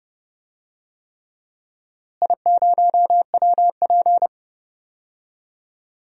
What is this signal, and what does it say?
Morse code, heard as a keyed tone, about 30 wpm, 700 Hz. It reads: I0WP
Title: DXCC-100 Call Signs - 4 Characters - Repeated 3x - 30wpm